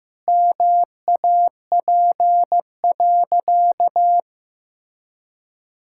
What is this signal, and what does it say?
Morse code, heard as a keyed tone, about 15 wpm, 700 Hz. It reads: MAP.
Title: Sentences from Top 1000 Words - Review Part 6 - 15wpm